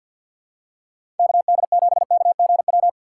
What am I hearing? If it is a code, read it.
Morse code, heard as a keyed tone, about 53 wpm, 700 Hz. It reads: KD7KGW
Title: USA Call Signs - 6 Characters - 4x Character Spacing - 50wpm